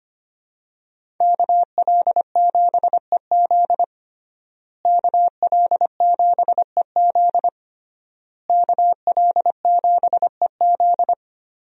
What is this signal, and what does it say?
Morse code, heard as a keyed tone, about 25 wpm, 700 Hz. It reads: KL7EZ KL7EZ KL7EZ
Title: DXCC-100 Call Signs - 5 Characters - Repeated 3x - 25wpm